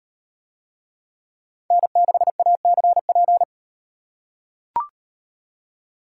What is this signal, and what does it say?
Morse code, heard as a keyed tone, about 38 wpm, 700 Hz. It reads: N6ACP